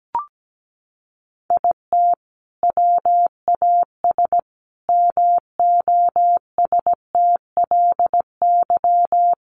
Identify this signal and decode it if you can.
Morse code, heard as a keyed tone, about 17 wpm, 700 Hz. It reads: IT WAS MOSTLY
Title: Sentences from Top 1300 Words 17wpm